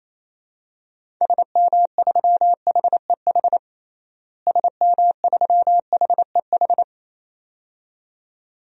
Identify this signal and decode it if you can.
Morse code, heard as a keyed tone, about 28 wpm, 700 Hz. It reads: SM3HEH SM3HEH
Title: DXCC-100 Call Signs - 6 Characters - Repeated 2x - 28wpm